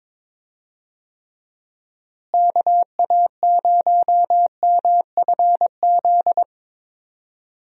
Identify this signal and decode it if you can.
Morse code, heard as a keyed tone, about 22 wpm, 700 Hz. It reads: KA0MFZ
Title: USA Call Signs - 6 Characters - Repeated 3x - 22wpm